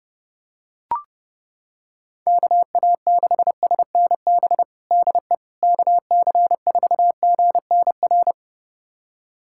Morse code, heard as a keyed tone, about 32 wpm, 700 Hz. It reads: KA6SNB DE KC4GNR
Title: USA Callsign DE Callsign - 6 Characters - 30wpm